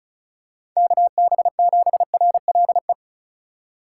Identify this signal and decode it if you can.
Morse code, heard as a keyed tone, about 36 wpm, 700 Hz. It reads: KB7RLE
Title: USA Call Signs - 6 Characters - 4x Character Spacing - 35wpm